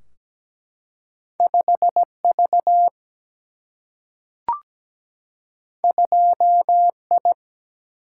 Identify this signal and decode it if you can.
Morse code, heard as a keyed tone, about 17 wpm, 700 Hz. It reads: 5 V 2 I
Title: Single-Letter Single-Number 17wpm